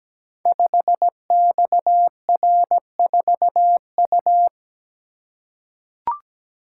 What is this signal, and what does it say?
Morse code, heard as a keyed tone, about 17 wpm, 700 Hz. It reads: 5XR4U
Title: Code Groups - 5 Letter:Numbers - 17wpm